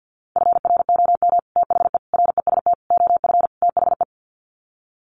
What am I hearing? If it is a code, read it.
Morse code, heard as a keyed tone, about 50 wpm, 700 Hz. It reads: FROM THE REST OF THE
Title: Sets of 5 Words 50wpm